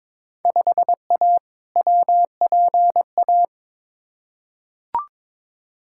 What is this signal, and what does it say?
Morse code, heard as a keyed tone, about 22 wpm, 700 Hz. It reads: 5A WPA
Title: Field Day Exchanges Encore - 22wpm